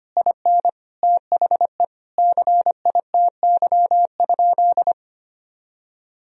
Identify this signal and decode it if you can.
Morse code, heard as a keyed tone, about 25 wpm, 700 Hz. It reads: IN THE CITY?
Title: Sentences from Top 1000 Words - Review Part 1 - 25wpm